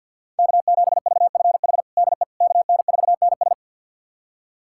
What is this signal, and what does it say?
Morse code, heard as a keyed tone, about 49 wpm, 700 Hz. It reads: K7VFH DE KN4NS
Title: USA Callsign DE Callsign - 5 Characters - 50wpm